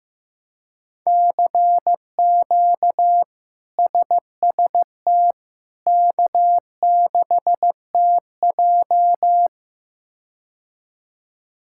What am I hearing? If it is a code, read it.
Morse code, heard as a keyed tone, about 16 wpm, 700 Hz. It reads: CQ SST K6TJ
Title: K1USN SST QSO Practice - 15wpm